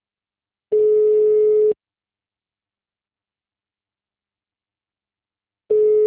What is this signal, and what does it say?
Ringback tone (Europe, 425 Hz).